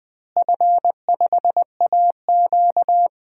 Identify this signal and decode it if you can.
Morse code, heard as a keyed tone, about 20 wpm, 700 Hz. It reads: F5AQ
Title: DXCC-100 Call Signs - 4 Characters - Repeated 3x - 20wpm